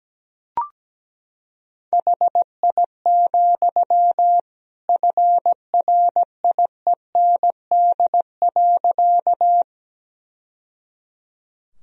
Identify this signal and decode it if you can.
Morse code, heard as a keyed tone, about 17 wpm, 700 Hz. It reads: HI, FRIEND.
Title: Sentences from Top 1000 Words - Review Part 3 - 17wpm